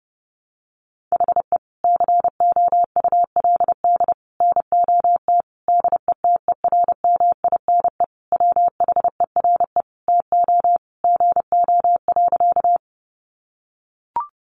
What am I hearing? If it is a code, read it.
Morse code, heard as a keyed tone, about 30 wpm, 700 Hz. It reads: HE COULD NOT DETERMINE WHERE TO GO.
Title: Sentences from Top 700 Words 30wpm